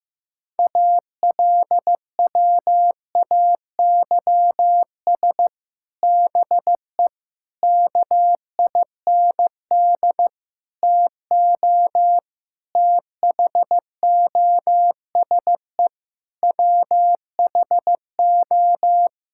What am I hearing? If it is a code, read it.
Morse code, heard as a keyed tone, about 14 wpm, 700 Hz. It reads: ALWAYS BE KIND TO THOSE WHO
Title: Sentences from Top 1600 Words 15wpm